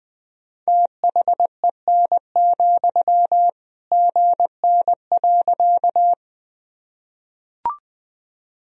Morse code, heard as a keyed tone, about 20 wpm, 700 Hz. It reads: THEN, GN.
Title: Sentences from Top 300 Words and Abbreviations 20wpm